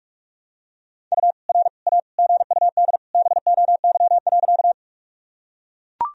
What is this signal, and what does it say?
Morse code, heard as a keyed tone, about 42 wpm, 700 Hz. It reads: U R A GUD BOY.
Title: Sentences from Top 200 Words and Abbreviations 45wpm